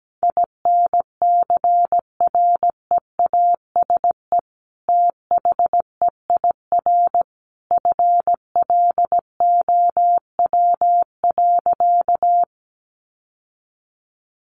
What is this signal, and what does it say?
Morse code, heard as a keyed tone, about 17 wpm, 700 Hz. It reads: INCREASE THEIR FLOW.
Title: Sentences from Top 1200 Words 17wpm